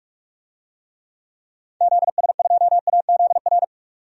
Morse code, heard as a keyed tone, about 42 wpm, 700 Hz. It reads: ZS2AZR
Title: DXCC-100 Call Signs - 6 Characters - 45wpm